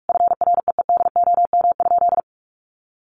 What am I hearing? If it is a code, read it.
Morse code, heard as a keyed tone, about 44 wpm, 700 Hz. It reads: FREEDOM?
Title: Sentences from Top 1500 Words 45wpm